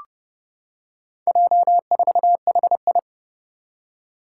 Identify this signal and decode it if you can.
Morse code, heard as a keyed tone, about 30 wpm, 700 Hz. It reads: J4HI
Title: Four Challenging Letter-Number Combinations 30wpm